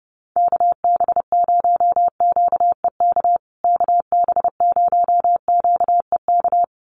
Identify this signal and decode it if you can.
Morse code, heard as a keyed tone, about 30 wpm, 700 Hz. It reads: KB0QEK KB0QEK
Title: USA Call Signs Repeated 2x 30wpm